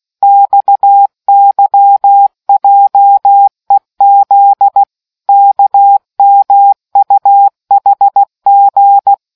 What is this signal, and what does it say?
Morse code, heard as a keyed tone, about 16 wpm, 800 Hz. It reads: XYJEZ KMUHG